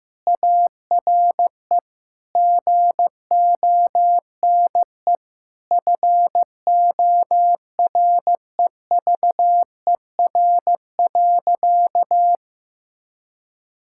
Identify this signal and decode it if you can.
Morse code, heard as a keyed tone, about 15 wpm, 700 Hz. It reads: ARE GONE FOREVER.
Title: Sentences from Top 1600 Words 15wpm